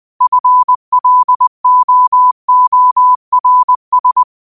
Morse code, heard as a keyed tone, about 20 wpm, 1000 Hz. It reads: FLOORS